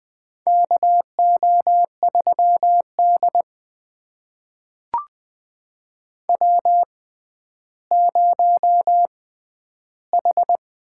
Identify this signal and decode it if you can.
Morse code, heard as a keyed tone, about 20 wpm, spaced as for 11 wpm, 700 Hz. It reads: KO3D W0H
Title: USA Call Signs - 4 Characters - 6x Character Spacing - 20wpm